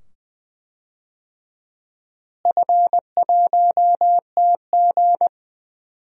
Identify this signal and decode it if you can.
Morse code, heard as a keyed tone, about 20 wpm, 700 Hz. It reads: F1TG
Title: DXCC-100 Call Signs - 4 Characters - Repeated 2x - 20wpm